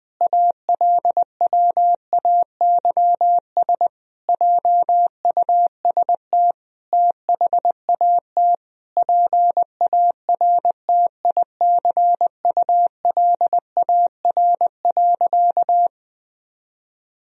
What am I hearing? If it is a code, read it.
Morse code, heard as a keyed tone, about 20 wpm, 700 Hz. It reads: ALWAYS JUST THAT PARTICULAR.
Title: Sentences from Top 1000 Words - Review Part 4 - 20wpm